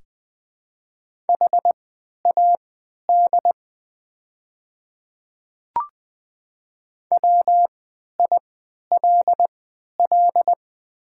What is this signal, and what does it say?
Morse code, heard as a keyed tone, about 20 wpm, spaced as for 11 wpm, 700 Hz. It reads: HAD WILL
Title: 3x Character Spacing - Top 100 Words - 20wpm